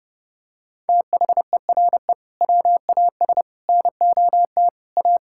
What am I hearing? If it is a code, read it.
Morse code, heard as a keyed tone, about 30 wpm, 700 Hz. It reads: THERE WAS NOT A
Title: Sentences from Top 1000 Words - Review Part 6 - 30wpm